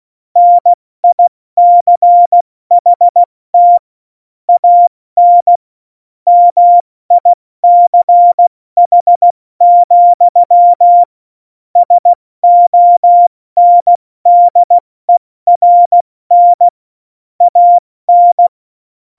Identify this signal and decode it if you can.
Morse code, heard as a keyed tone, about 16 wpm, spaced as for 14 wpm, 700 Hz. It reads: NICHT AN MICH, SONDERN AN
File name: Morsekurs / Witze / Witz 15. September 2020, 70 F80